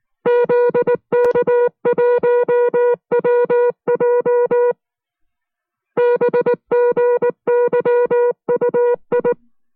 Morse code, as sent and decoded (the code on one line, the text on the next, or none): --.. -.- .---- .-- .--- / -... --. -.-- ..- ..
ZK1WJ BGYUI